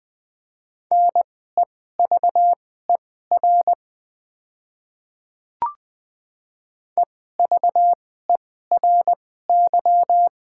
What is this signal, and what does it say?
Morse code, heard as a keyed tone, about 20 wpm, spaced as for 14 wpm, 700 Hz. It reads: NEVER EVERY
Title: 2x Character Spacing - Top 300 Words - 20wpm